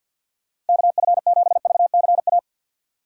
Morse code, heard as a keyed tone, about 47 wpm, 700 Hz. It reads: KF7VCA
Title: USA Call Signs - 6 Characters - 4x Character Spacing - 50wpm